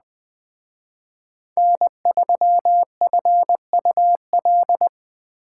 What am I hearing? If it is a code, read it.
Morse code, heard as a keyed tone, about 20 wpm, 700 Hz. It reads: N3FUL
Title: Speed Racing - USA 1x3 Call Signs 17wpm